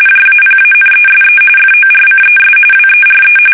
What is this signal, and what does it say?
RTTY, about 45.45 baud. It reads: JAPANESE FORCES WERE
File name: fsk850 / battleshipnj